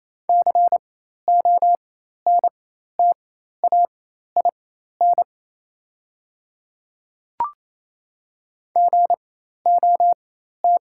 Morse code, heard as a keyed tone, about 28 wpm, spaced as for 13 wpm, 700 Hz. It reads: CONTAIN GOT
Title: 4x Character Spacing - Top 500 Words - 28wpm